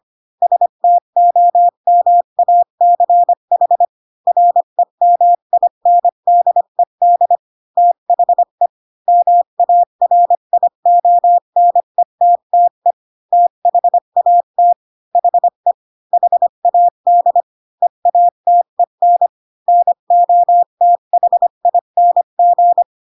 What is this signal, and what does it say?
Morse code, heard as a keyed tone, about 25 wpm, 700 Hz. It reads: STOMACH REMINDED THE MARIONETTE THAT HE HAD EATEN NOTHING
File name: Pinocchio / Pinocchio ch5 20 25 ik1hge